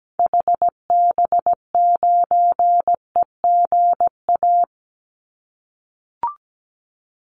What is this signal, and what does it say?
Morse code, heard as a keyed tone, about 17 wpm, 700 Hz. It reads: HB9EGA
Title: Call Signs - 17wpm